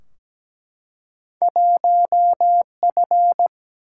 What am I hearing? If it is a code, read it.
Morse code, heard as a keyed tone, about 17 wpm, 700 Hz. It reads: 1F
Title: Single-Letter Single-Number 17wpm